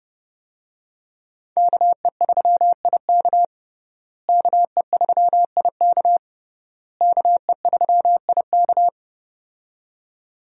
Morse code, heard as a keyed tone, about 30 wpm, 700 Hz. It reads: KE3IK KE3IK KE3IK
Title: USA Call Signs - 5 Characters - Repeated 3x - 30wpm